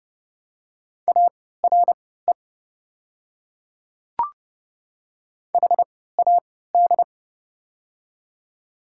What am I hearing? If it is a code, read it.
Morse code, heard as a keyed tone, about 30 wpm, spaced as for 17 wpm, 700 Hz. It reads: ARE HAD
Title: 3x Character Spacing - Top 100 Words - 30wpm